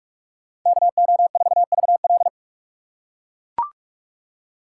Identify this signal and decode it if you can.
Morse code, heard as a keyed tone, about 41 wpm, 700 Hz. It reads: KO4VL